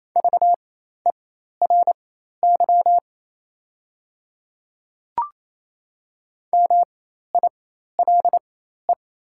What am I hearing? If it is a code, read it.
Morse code, heard as a keyed tone, about 28 wpm, spaced as for 13 wpm, 700 Hz. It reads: VERY MILE
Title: 4x Character Spacing - Top 500 Words - 28wpm